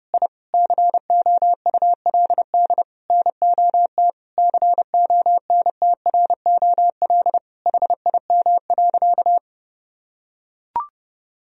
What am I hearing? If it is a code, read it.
Morse code, heard as a keyed tone, about 29 wpm, 700 Hz. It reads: I COULD NOT CONTROL HIM.
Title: Sentences from Top 1000 Words - Review Part 3 - 30wpm